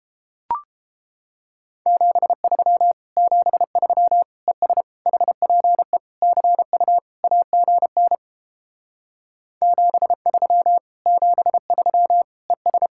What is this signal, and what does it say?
Morse code, heard as a keyed tone, about 31 wpm, 700 Hz. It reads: 73 73 ES HPE CU AGN 73 73 ES